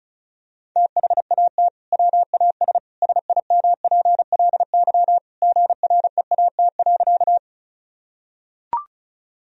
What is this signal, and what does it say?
Morse code, heard as a keyed tone, about 36 wpm, 700 Hz. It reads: THAT WAS SIMPLY GREAT.
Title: Sentences from Top 1000 Words - Review Part 4 - 35wpm